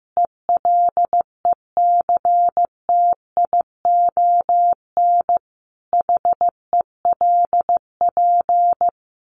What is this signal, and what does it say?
Morse code, heard as a keyed tone, about 15 wpm, 700 Hz. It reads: ELECTION HELP
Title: Sentences from Top 1500 Words 15wpm